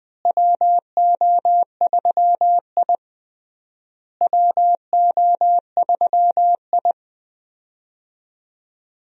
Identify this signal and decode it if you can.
Morse code, heard as a keyed tone, about 20 wpm, 700 Hz. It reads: WO3I WO3I
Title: USA Call Signs - 4 Characters - Repeated 3x - 20wpm